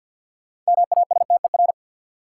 Morse code, heard as a keyed tone, about 51 wpm, 700 Hz. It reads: MASTER
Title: Sentences from Top 1300 Words - New Word Drill - 50wpm